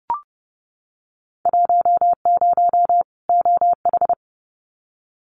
Morse code, heard as a keyed tone, about 30 wpm, 700 Hz. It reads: 10 OH
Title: Winter Field Day Exchanges 30wpm